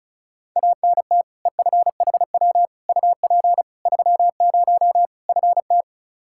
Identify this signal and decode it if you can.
Morse code, heard as a keyed tone, about 36 wpm, 700 Hz. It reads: ANT EFHW UP 30 FT
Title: Practice QSO - Exchange 2 - Info - 35wpm